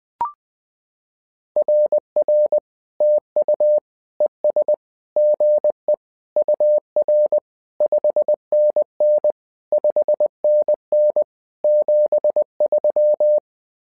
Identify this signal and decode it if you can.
Morse code, heard as a keyed tone, about 20 wpm, 600 Hz. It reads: RR TU ES GE UR 5NN 5NN 73
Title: SOTA QSO Practice 20wpm